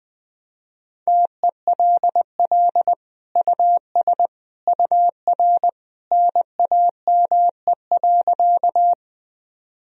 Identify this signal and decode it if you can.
Morse code, heard as a keyed tone, about 20 wpm, 700 Hz. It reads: TELL US UR NAME.
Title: Sentences from Top 200 Words and Abbreviations 20wpm